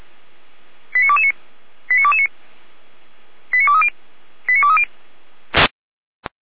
Five-tone selective call (ZVEI), heard as quick runs of five tones, tones of about 70 ms. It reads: 89229 89229 89230 89230